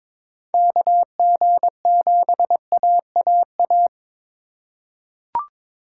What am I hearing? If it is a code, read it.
Morse code, heard as a keyed tone, about 22 wpm, 700 Hz. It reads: KG7AAA
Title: USA Call Signs - 2x Character Spacing - 22wpm